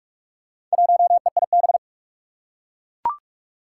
Morse code, heard as a keyed tone, about 44 wpm, 700 Hz. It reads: 1EIB
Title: Four Challenging Letter-Number Combinations 45wpm